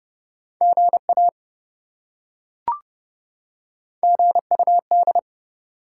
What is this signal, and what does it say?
Morse code, heard as a keyed tone, about 26 wpm, 700 Hz. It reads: GA GUD